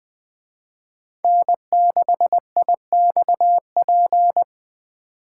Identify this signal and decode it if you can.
Morse code, heard as a keyed tone, about 20 wpm, 700 Hz. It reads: N6IXP